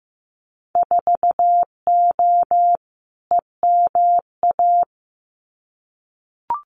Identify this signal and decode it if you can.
Morse code, heard as a keyed tone, about 15 wpm, 700 Hz. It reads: 4O EMA
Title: Winter Field Day Exchanges Encore 15wpm